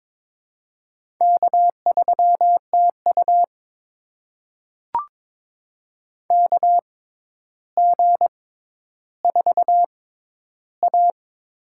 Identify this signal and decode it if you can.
Morse code, heard as a keyed tone, about 22 wpm, spaced as for 11 wpm, 700 Hz. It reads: K3TU KG4A
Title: USA Call Signs - 4 Characters - 6x Character Spacing - 22wpm